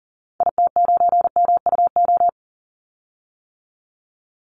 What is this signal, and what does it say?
Morse code, heard as a keyed tone, about 40 wpm, 700 Hz. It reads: IT9MUO
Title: Call Signs - 40wpm